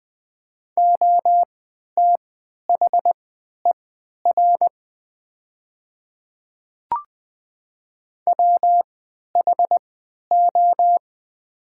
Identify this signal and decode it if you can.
Morse code, heard as a keyed tone, about 20 wpm, spaced as for 11 wpm, 700 Hz. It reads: OTHER WHO
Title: 3x Character Spacing - Top 100 Words - 20wpm